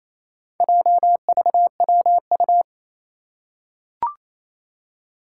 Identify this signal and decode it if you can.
Morse code, heard as a keyed tone, about 28 wpm, 700 Hz. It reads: JVWU
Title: Four Challenging Letter-Number Combinations 28wpm